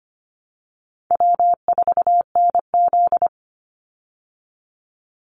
Morse code, heard as a keyed tone, about 25 wpm, 700 Hz. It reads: W4NZ
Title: Easy Call Signs Repeated 3x - 25wpm